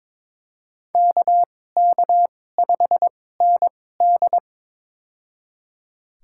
Morse code, heard as a keyed tone, about 22 wpm, spaced as for 16 wpm, 700 Hz. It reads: KK5ND
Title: USA Call Signs - 5 Characters - 2x Character Spacing - 22wpm